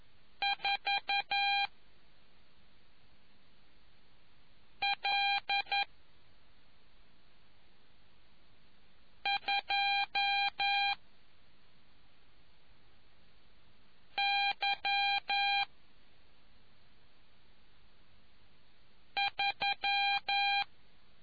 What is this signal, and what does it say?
Morse code, heard as a keyed tone, about 11 wpm, 800 Hz. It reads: 4 L 2 Y 3